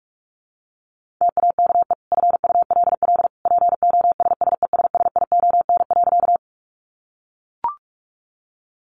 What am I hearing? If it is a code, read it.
Morse code, heard as a keyed tone, about 44 wpm, 700 Hz. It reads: TAKE FULL POSSESSION.
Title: Sentences from Top 1400 Words 45wpm